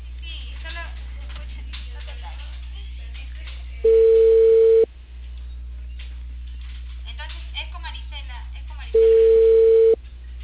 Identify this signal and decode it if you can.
Ringback tone (China, 450 Hz), with speech.